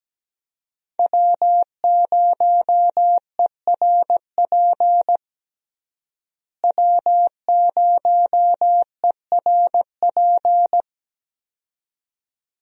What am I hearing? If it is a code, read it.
Morse code, heard as a keyed tone, about 17 wpm, 700 Hz. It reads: W0ERP W0ERP
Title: USA Call Signs - 5 Characters - Repeated 2x - 17wpm